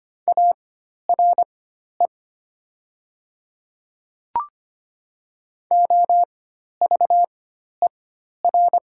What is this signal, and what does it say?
Morse code, heard as a keyed tone, about 25 wpm, spaced as for 12 wpm, 700 Hz. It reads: ARE OVER
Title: 4x Character Spacing - Top 100 Words - 25wpm